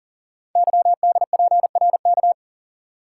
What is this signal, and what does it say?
Morse code, heard as a keyed tone, about 42 wpm, 700 Hz. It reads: YDPRK
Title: Code Groups - 5 Letters - 40wpm